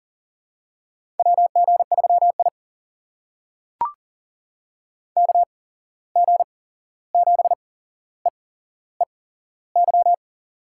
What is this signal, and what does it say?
Morse code, heard as a keyed tone, about 40 wpm, spaced as for 15 wpm, 700 Hz. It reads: WG3I KG7EEY
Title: USA Call Signs - 8x Character Spacing - 40wpm